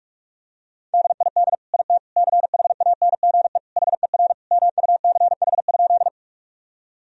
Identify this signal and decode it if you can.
Morse code, heard as a keyed tone, about 46 wpm, 700 Hz. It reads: DID IT CHANGE HER MUCH?
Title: Sentences from Top 1000 Words - Review Part 1 - 45wpm